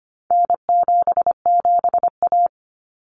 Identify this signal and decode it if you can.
Morse code, heard as a keyed tone, about 25 wpm, 700 Hz. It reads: N77A
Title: Common Cut Numbers - 4-Digit 25wpm